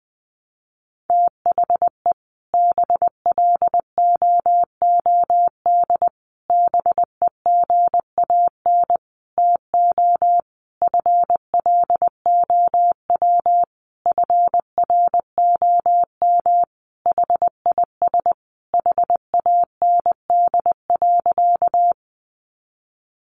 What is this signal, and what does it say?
Morse code, heard as a keyed tone, about 20 wpm, 700 Hz. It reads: THE BLOOD BEGAN TO FLOW FROM HIS HAND.
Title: Sentences from Top 1200 Words 20wpm